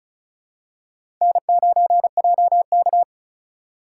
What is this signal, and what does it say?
Morse code, heard as a keyed tone, about 36 wpm, 700 Hz. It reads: N9JK
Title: USA Call Signs - 4 Characters - 2x Character Spacing - 35wpm